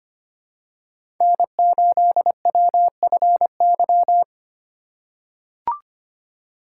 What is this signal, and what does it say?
Morse code, heard as a keyed tone, about 25 wpm, 700 Hz. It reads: N8WFY
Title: USA Call Signs - 5 Characters - 2x Character Spacing - 25wpm